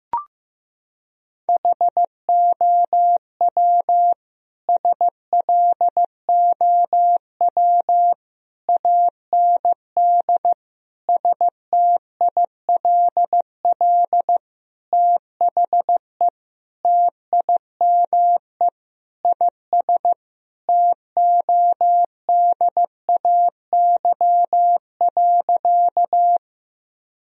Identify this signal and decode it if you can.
Morse code, heard as a keyed tone, about 16 wpm, 700 Hz. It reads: HOW SLOW AND STILL THE TIME IS TODAY.
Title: Sentences from Top 1000 Words - Review Part 2 - 15wpm